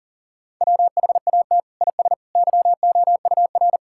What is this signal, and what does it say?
Morse code, heard as a keyed tone, about 40 wpm, 700 Hz. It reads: WHAT IS YOUR
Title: Sentences from Top 1000 Words - Review Part 6 - 40wpm